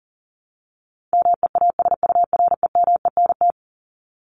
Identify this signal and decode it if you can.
Morse code, heard as a keyed tone, about 40 wpm, 700 Hz. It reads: MEASUREMENT
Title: Top 1751-2000 Words 40wpm